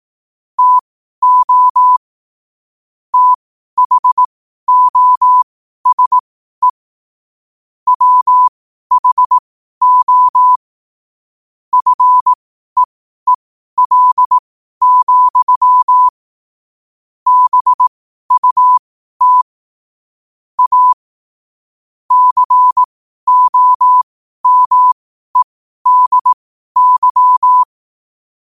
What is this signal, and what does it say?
Morse code, heard as a keyed tone, about 18 wpm, spaced as for 12 wpm, 1000 Hz. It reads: TO THOSE WHO FEEL, BUT A COMEDY
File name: qotdfiles / 12wpm